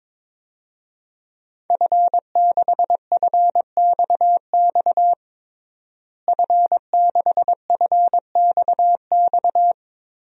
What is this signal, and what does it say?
Morse code, heard as a keyed tone, about 23 wpm, 700 Hz. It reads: F6FXX F6FXX
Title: DXCC-100 Call Signs - 5 Characters - Repeated 3x - 22wpm